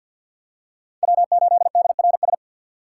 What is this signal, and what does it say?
Morse code, heard as a keyed tone, about 52 wpm, 700 Hz. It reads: W8DRS
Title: USA Call Signs - 5 Characters - 3x Character Spacing - 50wpm